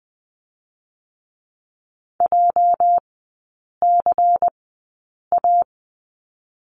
Morse code, heard as a keyed tone, about 20 wpm, spaced as for 8 wpm, 700 Hz. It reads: JCA
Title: 205 Lesson 33e - Letter j - Three Characters 20wpm